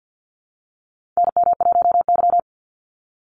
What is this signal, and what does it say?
Morse code, heard as a keyed tone, about 50 wpm, 700 Hz. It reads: NM1Y
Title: Call Signs - 50wpm